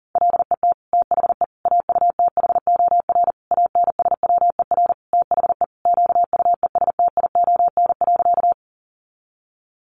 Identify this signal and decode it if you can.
Morse code, heard as a keyed tone, about 40 wpm, 700 Hz. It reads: LET THE AUTHOR ANSWER THE QUESTION.
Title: Sentences from Top 1200 Words 40wpm